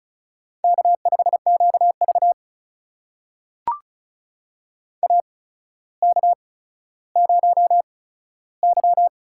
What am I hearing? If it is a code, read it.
Morse code, heard as a keyed tone, about 36 wpm, spaced as for 15 wpm, 700 Hz. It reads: K5QV AK0Y
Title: USA Call Signs - 4 Characters - 8x Character Spacing - 35wpm